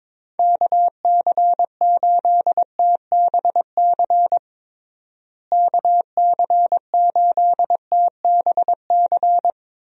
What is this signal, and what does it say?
Morse code, heard as a keyed tone, about 22 wpm, 700 Hz. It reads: KC8TBC KC8TBC
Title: USA Call Signs - 6 Characters - Repeated 2x - 22wpm